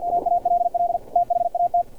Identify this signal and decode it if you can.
Morse code, heard as a keyed tone, about 50 wpm, 700 Hz, with radio noise. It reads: AALL THAT
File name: archived / CRSnet